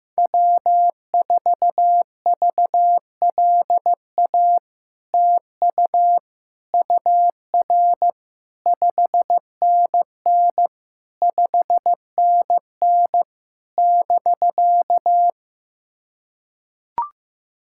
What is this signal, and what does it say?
Morse code, heard as a keyed tone, about 15 wpm, 700 Hz. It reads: W4VLA TU UR 5NN 5NN <BK>
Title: SOTA QSO Practice 15wpm